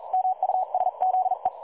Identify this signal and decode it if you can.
Morse code, heard as a keyed tone, about 40 wpm, 750 Hz, with radio noise. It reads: MUSZE
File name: dlazaawansowanych / nagraniazpasma / websdr recording 3565.0kHz